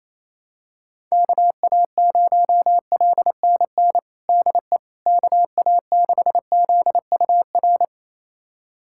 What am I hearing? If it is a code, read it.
Morse code, heard as a keyed tone, about 28 wpm, 700 Hz. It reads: KA0LNN DE KA6ZUR
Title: USA Callsign DE Callsign - 6 Characters - 28wpm